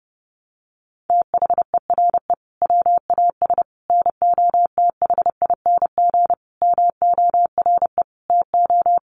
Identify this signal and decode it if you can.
Morse code, heard as a keyed tone, about 30 wpm, 700 Hz. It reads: THERE WAS NOTHING MORE TO
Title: Sentences from Top 1100 Words 30wpm